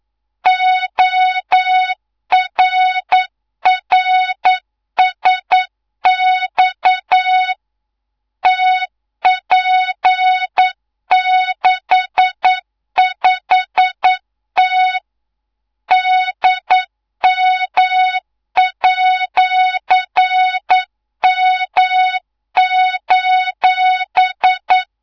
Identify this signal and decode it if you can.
Morse code, heard as a keyed tone, about 9 wpm, 3700 Hz, with noise. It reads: ORRSX TP65T DM@M:
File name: Morze-HA2MN / hangp / 2 03 GYAKORLAT